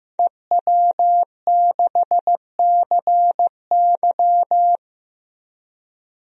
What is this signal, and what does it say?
Morse code, heard as a keyed tone, about 16 wpm, 700 Hz. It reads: EW6CY
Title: DXCC-100 Call Signs - 5 Characters - 15wpm